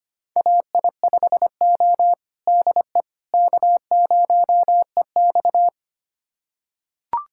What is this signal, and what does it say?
Morse code, heard as a keyed tone, about 25 wpm, 700 Hz. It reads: AI5O DE K0EX
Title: USA Callsign DE Callsign - 4 Characters - 25wpm